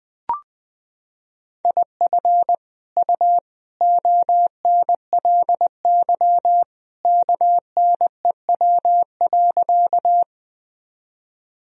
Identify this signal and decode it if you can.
Morse code, heard as a keyed tone, about 21 wpm, 700 Hz. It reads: IF U ONLY KNEW.
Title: Sentences from Top 400 Words and Abbreviations 20wpm